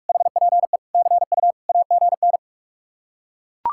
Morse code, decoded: HPE CU AGN

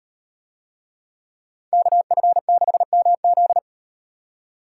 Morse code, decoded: KF6MZ

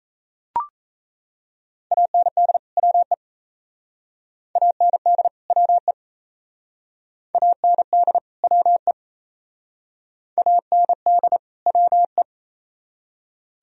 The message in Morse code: AND WE AND WE AND WE AND WE

